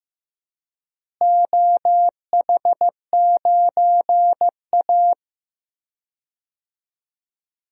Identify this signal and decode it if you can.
Morse code, heard as a keyed tone, about 15 wpm, 700 Hz. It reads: OH9A
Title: DXCC-100 Call Signs - 4 Characters - Repeated 2x - 15wpm